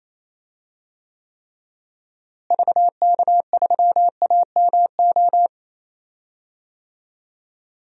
Morse code, heard as a keyed tone, about 28 wpm, 700 Hz. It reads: VK3AMO
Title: DXCC-100 Call Signs - 6 Characters - Repeated 3x - 28wpm